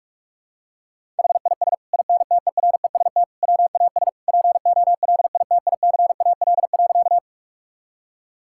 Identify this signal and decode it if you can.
Morse code, heard as a keyed tone, about 46 wpm, 700 Hz. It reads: HIS INTEREST WAS POLITICAL.
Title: Sentences from Top 1000 Words - Review Part 5 - 45wpm